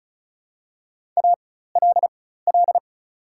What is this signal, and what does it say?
Morse code, heard as a keyed tone, about 36 wpm, spaced as for 16 wpm, 700 Hz. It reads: ALL
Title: 4x Character Spacing - Top 200 Words - 35wpm